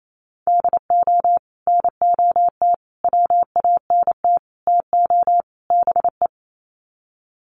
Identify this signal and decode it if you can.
Morse code, heard as a keyed tone, about 28 wpm, 700 Hz. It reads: DO NOT WANT TO BE
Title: Sets of 5 Words 28wpm